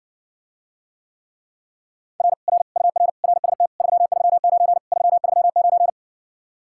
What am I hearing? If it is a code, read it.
Morse code, heard as a keyed tone, about 60 wpm, 700 Hz. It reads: R R UR RST 339 339